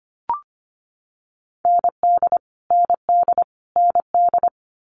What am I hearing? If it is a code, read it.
Morse code, heard as a keyed tone, about 25 wpm, 700 Hz. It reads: ND ND ND
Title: Multi-letter Phonemes Repeated 3x 25wpm